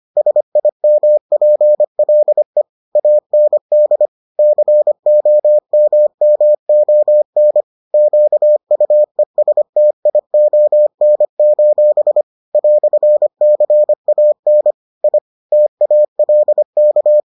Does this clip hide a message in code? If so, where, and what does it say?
Yes: Morse code from 0.2 to 17.2 s, heard as a keyed tone, about 25 wpm, 600 Hz. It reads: SIMPLE AND COMMON QUESTION: "CAN I TALK